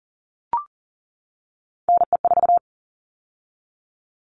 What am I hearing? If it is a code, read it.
Morse code, heard as a keyed tone, about 40 wpm, 700 Hz. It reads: NE4
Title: All Cut Numbers - 3-Digit 40wpm